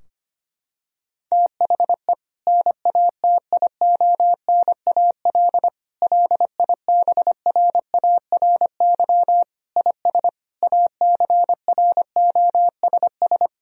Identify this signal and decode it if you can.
Morse code, heard as a keyed tone, about 25 wpm, 700 Hz. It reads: THE NATIONAL LIBRARY IS ACROSS
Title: Sentences from Top 1000 Words - Review Part 4 - 25wpm